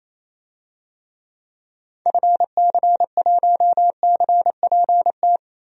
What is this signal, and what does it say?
Morse code, heard as a keyed tone, about 28 wpm, 700 Hz. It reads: FC1CPT
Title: DXCC-100 Call Signs - 6 Characters - 28wpm